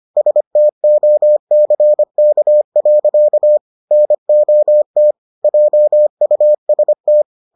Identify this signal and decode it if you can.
Morse code, heard as a keyed tone, about 25 wpm, 600 Hz. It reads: STOCK. NOT JUST